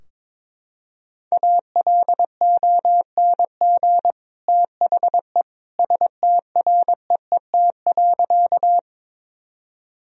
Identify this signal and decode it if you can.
Morse code, heard as a keyed tone, about 22 wpm, 700 Hz. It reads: ALONG THE STREET.
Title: Sentences from Top 1000 Words - Review Part 3 - 22wpm